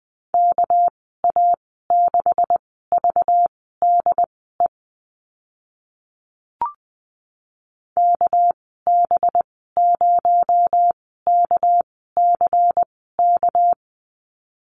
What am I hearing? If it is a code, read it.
Morse code, heard as a keyed tone, about 20 wpm, spaced as for 14 wpm, 700 Hz. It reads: KA6VDE KB0KCK
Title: 238 Lesson 38g - Letter q - Call Signs 20wpm